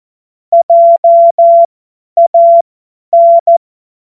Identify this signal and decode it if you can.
Morse code, heard as a keyed tone, about 14 wpm, spaced as for 10 wpm, 680 Hz. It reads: JAN
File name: Morsekurs / Witze / Witz 16. Februar 2021, 50 F70